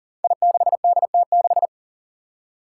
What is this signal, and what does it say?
Morse code, heard as a keyed tone, about 40 wpm, 700 Hz. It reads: I6DT6
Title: Five Challenging Letter-Number Combinations 40wpm